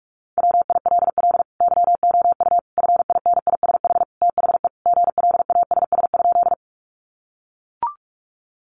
Morse code, heard as a keyed tone, about 44 wpm, 700 Hz. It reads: WILL YOU FINISH THE GLASS?
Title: Sentences from Top 1300 Words 45wpm